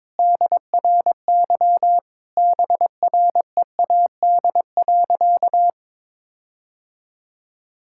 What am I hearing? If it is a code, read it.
Morse code, heard as a keyed tone, about 22 wpm, 700 Hz. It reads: DRY BREAD.
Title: Sentences from Top 1600 Words 22wpm